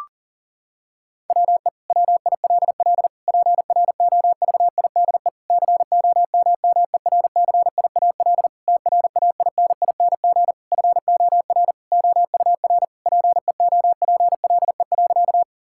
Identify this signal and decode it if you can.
Morse code, heard as a keyed tone, about 40 wpm, 700 Hz. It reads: WE WILL PROVIDE COMMERCIAL TRAINING FOR OUR PEOPLE.
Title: Sentences from Top 1000 Words - Review Part 5 - 40wpm